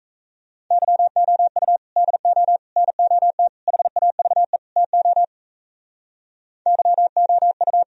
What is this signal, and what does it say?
Morse code, heard as a keyed tone, about 37 wpm, 700 Hz. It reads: YOU DO NOT HAVE TO YOU